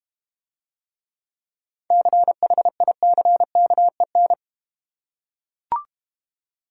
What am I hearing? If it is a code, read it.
Morse code, heard as a keyed tone, about 31 wpm, 700 Hz. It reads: CHICKEN
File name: Easy Words - Speed-Racing with Hints - 25wpm